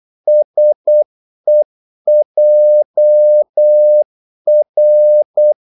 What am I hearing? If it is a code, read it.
Morse code, heard as a keyed tone, about 8 wpm, 600 Hz. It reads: SEJR
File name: Korte ord DK 8wpm